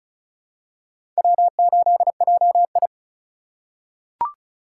Morse code, heard as a keyed tone, about 36 wpm, 700 Hz. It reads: W8JI
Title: USA Call Signs - 4 Characters - 3x Character Spacing - 35wpm